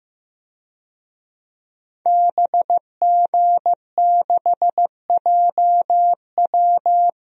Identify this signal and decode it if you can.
Morse code, heard as a keyed tone, about 15 wpm, 700 Hz. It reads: BG6JW
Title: DXCC-100 Call Signs - 5 Characters - 15wpm